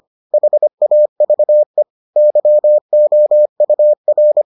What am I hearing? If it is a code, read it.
Morse code, heard as a keyed tone, about 25 wpm, 600 Hz. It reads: HAVE YOUR